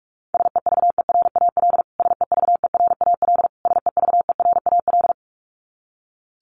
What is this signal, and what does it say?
Morse code, heard as a keyed tone, about 44 wpm, 700 Hz. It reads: SEVERAL SEVERAL SEVERAL
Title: Top 500 Words Repeated 45wpm